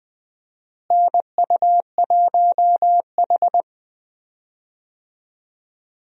Morse code, heard as a keyed tone, about 21 wpm, 700 Hz. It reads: NU1H
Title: USA Call Signs - 4 Characters - Repeated 3x - 20wpm